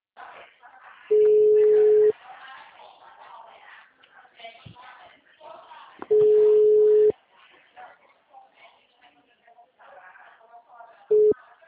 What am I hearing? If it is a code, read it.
Ringback tone (Europe, 425 Hz), with speech.
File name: out- -2007- - - .58330